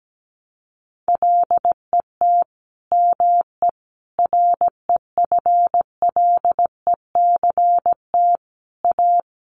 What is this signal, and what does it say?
Morse code, heard as a keyed tone, about 17 wpm, 700 Hz. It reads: LET ME REFLECT A
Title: Sentences from Top 1500 Words 17wpm